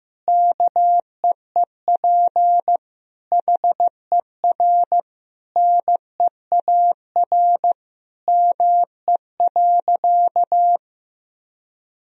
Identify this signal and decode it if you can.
Morse code, heard as a keyed tone, about 15 wpm, 700 Hz. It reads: KEEP HER NEAR ME.
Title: Sentences from Top 1000 Words - Review Part 1 - 15wpm